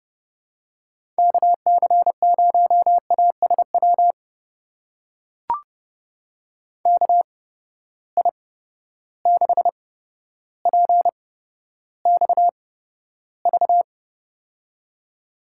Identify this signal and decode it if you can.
Morse code, heard as a keyed tone, about 31 wpm, spaced as for 13 wpm, 700 Hz. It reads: KC0ASW KI6PXV
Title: USA Call Signs - 6 Characters - 8x Character Spacing - 30wpm